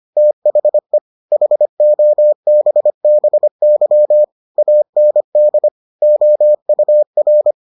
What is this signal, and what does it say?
Morse code, heard as a keyed tone, about 25 wpm, 600 Hz. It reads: THE HOBBY AND OUR